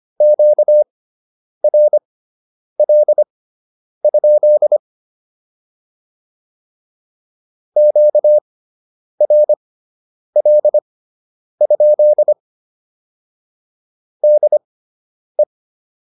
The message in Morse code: QRL? QRL? DE

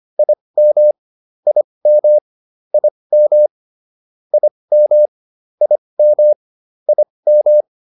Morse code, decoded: IM IM IM IM IM IM

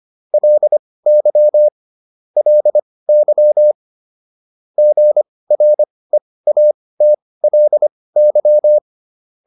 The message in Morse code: LY LY GREATLY

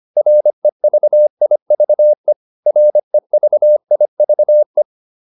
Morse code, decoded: REVIVE REVIVE

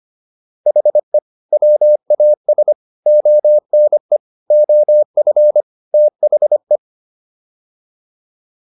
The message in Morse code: HE WAS ONE OF THE

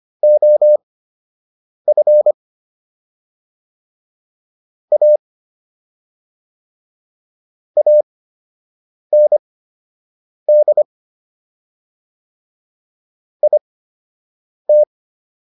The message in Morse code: OF A AND IT

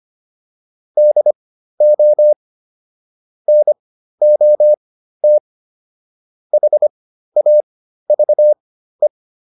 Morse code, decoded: DO NOT HAVE

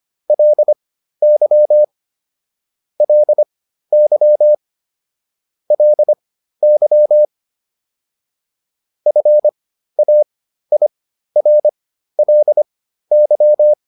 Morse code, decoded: LY LY LY FAIRLY

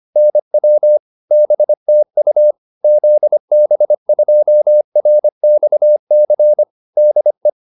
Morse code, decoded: NW BTU ZB2RXC DE

